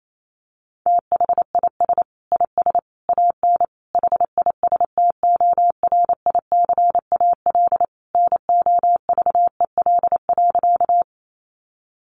Morse code, heard as a keyed tone, about 28 wpm, 700 Hz. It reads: THIS IS AN HISTORICAL NOVEL.
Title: Sentences from Top 1300 Words 28wpm